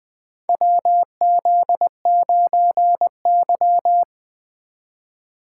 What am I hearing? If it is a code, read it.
Morse code, heard as a keyed tone, about 20 wpm, 700 Hz. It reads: WZ9Y
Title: USA Call Signs - 4 Characters - 2x Character Spacing - 20wpm